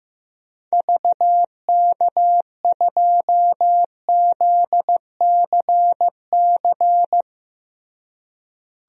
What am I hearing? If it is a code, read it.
Morse code, heard as a keyed tone, about 15 wpm, 700 Hz. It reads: VK2ZCC
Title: DXCC-100 Call Signs - 6 Characters - Repeated 2x - 15wpm